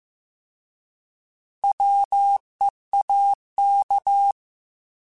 Morse code, heard as a keyed tone, about 15 wpm, 800 Hz. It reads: WEAK
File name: Alphabetical / Alphabetical mp3s / Ws @13